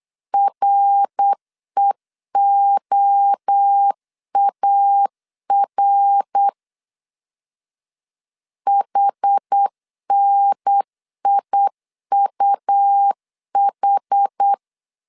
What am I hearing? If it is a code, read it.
Morse code, heard as a keyed tone, about 8 wpm, 800 Hz. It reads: REOAR HNIUH